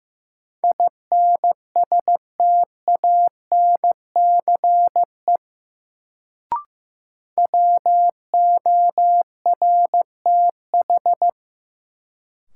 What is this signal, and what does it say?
Morse code, heard as a keyed tone, about 15 wpm, 700 Hz. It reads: INSTANCE WORTH
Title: Sentences from Top 800 Words - New Word Drill - 15wpm